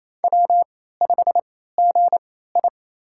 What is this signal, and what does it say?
Morse code, heard as a keyed tone, about 28 wpm, spaced as for 16 wpm, 700 Hz. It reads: W5GI
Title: USA Call Signs - 4 Characters - 3x Character Spacing - 28wpm